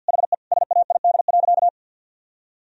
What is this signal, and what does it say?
Morse code, heard as a keyed tone, about 51 wpm, 700 Hz. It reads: HE SAID.
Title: Sentences from Top 1000 Words - Review Part 1 - 50wpm